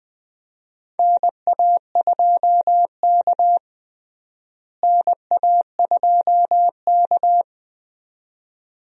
Morse code, heard as a keyed tone, about 20 wpm, 700 Hz. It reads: NA2K NA2K
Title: USA Call Signs - 4 Characters - Repeated 2x - 20wpm